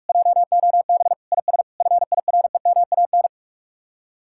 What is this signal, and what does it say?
Morse code, heard as a keyed tone, about 45 wpm, 700 Hz. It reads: JOB IS FIREMAN